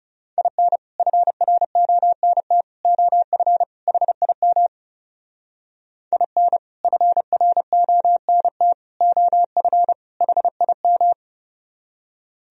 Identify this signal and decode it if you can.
Morse code, heard as a keyed tone, about 38 wpm, 700 Hz. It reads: IN FRONT OF HIM IN FRONT OF HIM